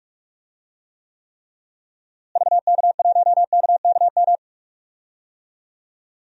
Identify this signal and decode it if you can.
Morse code, heard as a keyed tone, about 41 wpm, 700 Hz. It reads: VK1KKM
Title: DXCC-100 Call Signs - 6 Characters - Repeated 3x - 45wpm